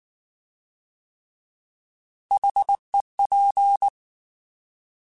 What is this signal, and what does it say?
Morse code, heard as a keyed tone, about 19 wpm, 800 Hz. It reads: HEP